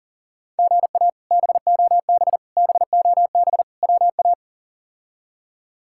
Morse code, heard as a keyed tone, about 40 wpm, 690 Hz. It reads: GA BOB BOB WA
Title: K1USN SST QSO Practice - 40wpm